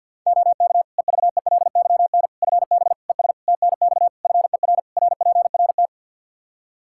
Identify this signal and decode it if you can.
Morse code, heard as a keyed tone, about 52 wpm, 700 Hz. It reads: OK EVELYN FB ES TNX FER RPRT
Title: Practice QSO - Exchange 2 - Info - 50wpm